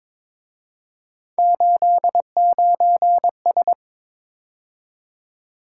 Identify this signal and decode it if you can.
Morse code, heard as a keyed tone, about 26 wpm, 700 Hz. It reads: 89S
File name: Speed-Racing - Three Letter-Number Combinations - 22wpm